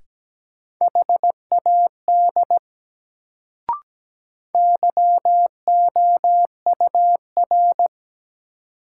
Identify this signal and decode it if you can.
Morse code, heard as a keyed tone, about 17 wpm, 700 Hz. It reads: HAD YOUR
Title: Sets of 2 Words - New Word Drill - 17wpm